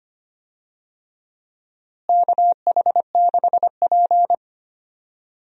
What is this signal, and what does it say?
Morse code, heard as a keyed tone, about 25 wpm, 700 Hz. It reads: KH6P
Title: DXCC-100 Call Signs - 4 Characters - Repeated 3x - 25wpm